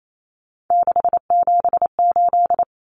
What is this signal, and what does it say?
Morse code, heard as a keyed tone, about 28 wpm, 700 Hz. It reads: 678
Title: Common Cut Numbers - 3-Digit 28wpm